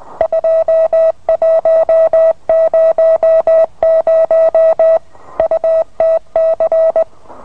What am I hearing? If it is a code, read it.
Morse code, heard as a keyed tone, about 20 wpm, 650 Hz. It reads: 2100 UTC